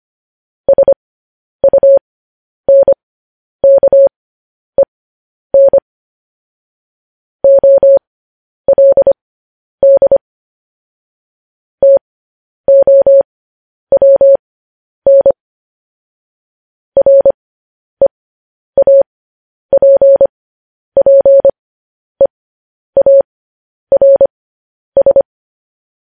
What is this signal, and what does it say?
Morse code, heard as a keyed tone, about 25 wpm, spaced as for 10 wpm, 550 Hz. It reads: SUNKEN OLD TOWN REAPPEARS